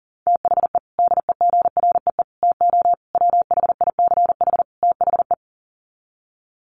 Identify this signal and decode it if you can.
Morse code, heard as a keyed tone, about 37 wpm, 700 Hz. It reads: THE DEGREE TO WHICH THE
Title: Sets of 5 Words Encore2 40wpm